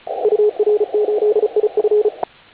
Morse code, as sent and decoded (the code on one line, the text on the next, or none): ..- .-. ---.. .. ..-.
UR8IF